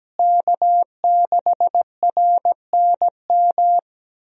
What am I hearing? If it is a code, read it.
Morse code, heard as a keyed tone, about 17 wpm, 700 Hz. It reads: K6RNM
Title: USA Call Signs - 5 Characters - Repeated 3x - 17wpm